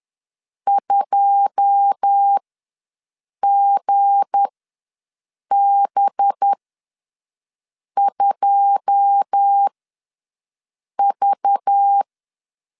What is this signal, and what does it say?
Morse code, heard as a keyed tone, about 10 wpm, spaced as for 5 wpm, 800 Hz. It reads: 2GB2V
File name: LEZIONI / 9 2 ws 10wpm 4130